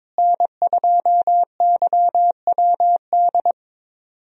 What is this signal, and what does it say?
Morse code, heard as a keyed tone, about 22 wpm, 700 Hz. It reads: N2YWD
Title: USA Call Signs - 5 Characters - Repeated 3x - 22wpm